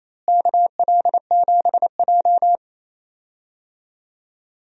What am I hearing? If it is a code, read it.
Morse code, heard as a keyed tone, about 28 wpm, 700 Hz. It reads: KL7J
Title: DXCC-100 Call Signs - 4 Characters - Repeated 3x - 28wpm